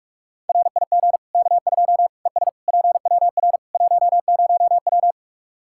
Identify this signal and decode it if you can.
Morse code, heard as a keyed tone, about 44 wpm, 700 Hz. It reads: RIG K2 ES PWR 10W
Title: Practice QSO - Exchange 2 - Info - 45wpm